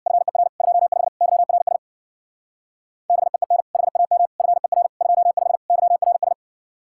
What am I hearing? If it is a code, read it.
Morse code, heard as a keyed tone, about 62 wpm, 700 Hz. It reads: FER 25 YRS BEEN HAM FER 25 YRS
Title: Speed-Racing Common QSO Phrases - 50wpm